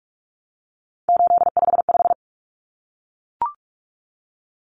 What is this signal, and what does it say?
Morse code, heard as a keyed tone, about 44 wpm, 700 Hz. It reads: 855
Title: Common Cut Numbers - 3-Digit 45wpm